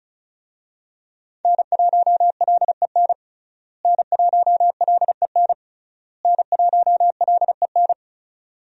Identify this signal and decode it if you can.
Morse code, heard as a keyed tone, about 34 wpm, 700 Hz. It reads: N1LEN N1LEN N1LEN
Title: USA Call Signs - 5 Characters - Repeated 3x - 35wpm